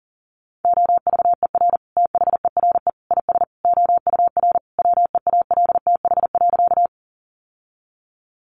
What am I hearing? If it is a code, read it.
Morse code, heard as a keyed tone, about 40 wpm, 700 Hz. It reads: OVER THERE IS OUR WEALTH.
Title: Sentences from Top 1200 Words 40wpm